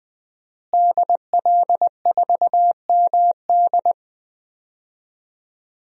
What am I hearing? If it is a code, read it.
Morse code, heard as a keyed tone, about 20 wpm, 700 Hz. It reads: DL4MD
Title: DXCC-100 Call Signs - 5 Characters - Repeated 3x - 20wpm